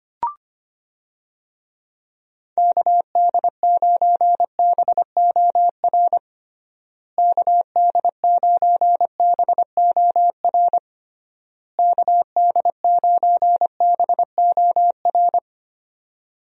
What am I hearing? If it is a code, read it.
Morse code, heard as a keyed tone, about 25 wpm, 700 Hz. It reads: KD9BOR KD9BOR KD9BOR
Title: USA Call Signs - 6 Characters - Repeated 3x - 25wpm